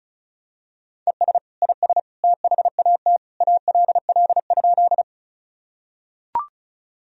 Morse code, heard as a keyed tone, about 34 wpm, 700 Hz. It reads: ES IS THAT ALL?
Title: Sentences from Top 100 Words and Abbreviations 35wpm